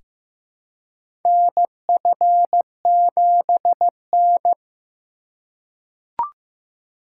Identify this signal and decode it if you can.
Morse code, heard as a keyed tone, about 15 wpm, 700 Hz. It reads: NF7N
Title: USA Call Signs - 4 Characters - 2x Character Spacing - 15wpm